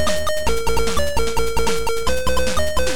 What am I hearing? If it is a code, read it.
There is no code.